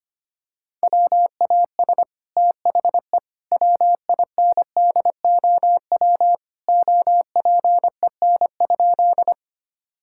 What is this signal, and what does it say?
Morse code, heard as a keyed tone, about 25 wpm, 700 Hz. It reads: WAS THE WINDOW OPEN?
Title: Sentences from Top 1600 Words 25wpm